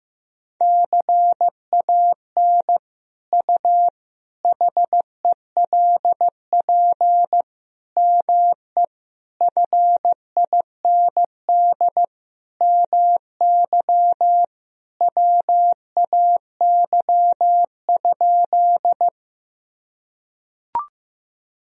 Morse code, heard as a keyed tone, about 14 wpm, 700 Hz. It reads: CAN U HELP ME FIND MY WAY?
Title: Sentences from Top 200 Words and Abbreviations 15wpm